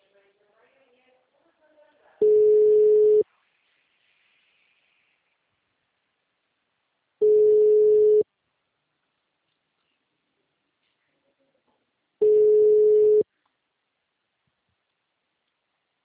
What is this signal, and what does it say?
Ringback tone (Europe, 425 Hz), with speech.